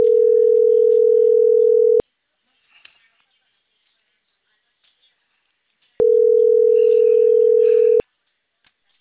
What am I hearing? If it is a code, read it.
Ringback tone (North America), 440 + 480 Hz.